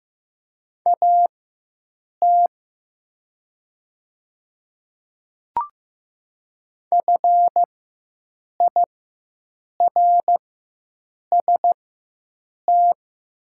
Morse code, heard as a keyed tone, about 15 wpm, spaced as for 7 wpm, 700 Hz. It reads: AT FIRST
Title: 4x Character Spacing - Top 100 Words - 15wpm